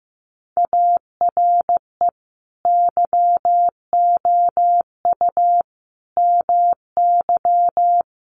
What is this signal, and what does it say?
Morse code, heard as a keyed tone, about 15 wpm, 700 Hz. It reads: ARE YOU MY
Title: Sentences from Top 1400 Words 15wpm